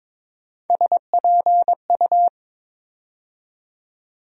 Morse code, heard as a keyed tone, about 22 wpm, 700 Hz. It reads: SPU
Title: USA Call Sign Postfixes - 3 Letters 22wpm